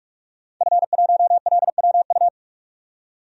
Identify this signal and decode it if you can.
Morse code, heard as a keyed tone, about 41 wpm, 700 Hz. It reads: F1LWU
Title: DXCC-100 Call Signs - 5 Characters - 45wpm